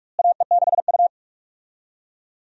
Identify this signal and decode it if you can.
Morse code, heard as a keyed tone, about 45 wpm, 700 Hz. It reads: AE6U
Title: USA Call Signs - 4 Characters - Repeated 2x - 45wpm